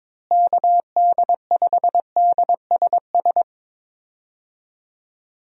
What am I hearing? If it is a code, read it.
Morse code, heard as a keyed tone, about 22 wpm, 700 Hz. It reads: KD5DSS
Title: USA Call Signs - 6 Characters - Repeated 2x - 22wpm